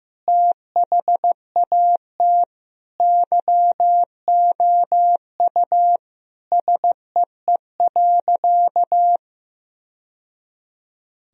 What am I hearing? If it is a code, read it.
Morse code, heard as a keyed tone, about 15 wpm, 700 Hz. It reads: THAT YOU SEE.
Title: Sentences from Top 1000 Words - Review Part 2 - 15wpm